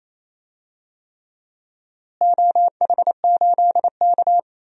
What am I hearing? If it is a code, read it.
Morse code, heard as a keyed tone, about 28 wpm, 700 Hz. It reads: OH8K
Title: DXCC-100 Call Signs - 4 Characters - Repeated 3x - 28wpm